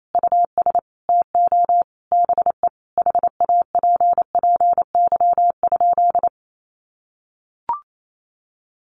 Morse code, heard as a keyed tone, about 28 wpm, 700 Hz. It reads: US TO BE HAPPY?
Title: Sentences from Top 1100 Words 28wpm